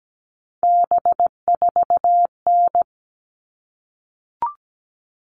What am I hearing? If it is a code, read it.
Morse code, heard as a keyed tone, about 17 wpm, 700 Hz. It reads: B4N
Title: All Cut Numbers - 3-Digit 17wpm